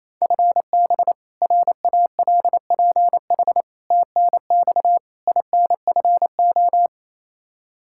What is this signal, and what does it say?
Morse code, heard as a keyed tone, about 28 wpm, 700 Hz. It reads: FB RALPH TNX INFO
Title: Practice QSO - Exchange 3 - More Info - 28wpm